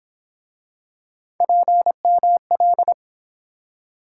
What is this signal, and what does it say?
Morse code, heard as a keyed tone, about 29 wpm, 700 Hz. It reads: PML